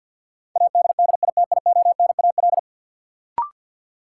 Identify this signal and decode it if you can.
Morse code, heard as a keyed tone, about 50 wpm, 700 Hz. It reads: ADDITIONAL
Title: Sentences from Top 1000 Words - Review Part 4 - New Word Drill - 50wpm